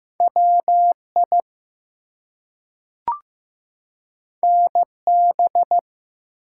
Morse code, heard as a keyed tone, about 15 wpm, 700 Hz. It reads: WI NB
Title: US States and Canadian Provinces 15wpm